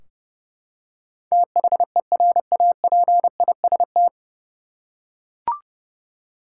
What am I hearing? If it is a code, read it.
Morse code, heard as a keyed tone, about 30 wpm, 700 Hz. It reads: THERAPIST
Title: K1USN SST QSO Practice 30wpm